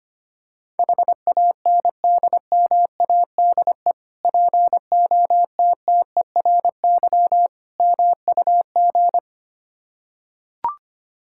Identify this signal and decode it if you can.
Morse code, heard as a keyed tone, about 25 wpm, 700 Hz. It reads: HANDMADE POTTERY MUG
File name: practice / 25wpm / Difficult Words with Hints - 25wpm